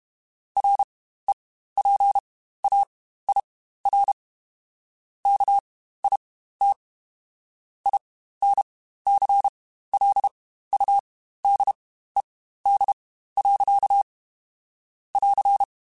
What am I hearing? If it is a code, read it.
Morse code, heard as a keyed tone, about 32 wpm, spaced as for 15 wpm, 800 Hz. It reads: REPAIR KIT INCLUDED. <AR>
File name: Quotes Spaced / Quotes Spaced-mp3s / Airtight sp @30